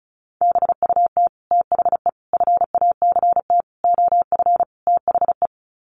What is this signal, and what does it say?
Morse code, heard as a keyed tone, about 34 wpm, 700 Hz. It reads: BUT THE FACT OF THE
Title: Sets of 5 Words Encore2 35wpm